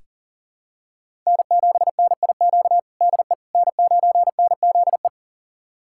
Morse code, heard as a keyed tone, about 40 wpm, 700 Hz. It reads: N7NIQ DE N9NZE
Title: USA Callsign DE Callsign - 5 Characters - 40wpm